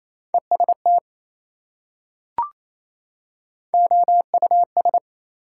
Morse code, heard as a keyed tone, about 28 wpm, 700 Hz. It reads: EST OUS
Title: Prefixes and Suffixes 28wpm